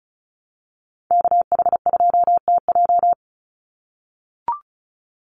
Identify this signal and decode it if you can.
Morse code, heard as a keyed tone, about 34 wpm, 700 Hz. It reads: KH2TJ
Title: NA SOTA Chaser Callsigns 35wpm